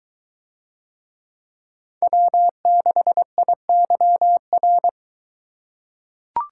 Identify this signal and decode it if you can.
Morse code, heard as a keyed tone, about 23 wpm, 700 Hz. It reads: W6IYR